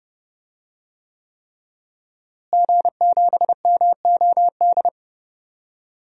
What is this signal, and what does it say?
Morse code, heard as a keyed tone, about 30 wpm, 700 Hz. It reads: G7MOD
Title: DXCC-100 Call Signs - 5 Characters - Repeated 3x - 30wpm